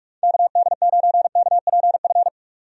Morse code, heard as a keyed tone, about 46 wpm, 700 Hz. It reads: KD9KPF
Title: USA Call Signs - 6 Characters - 45wpm